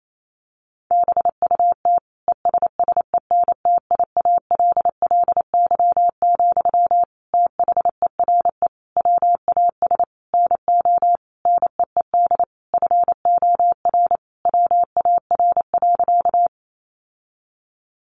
Morse code, heard as a keyed tone, about 28 wpm, 700 Hz. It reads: BUT ESSENTIALLY, THERE WAS NO NEED FOR WAR.
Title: Sentences from Top 1300 Words 28wpm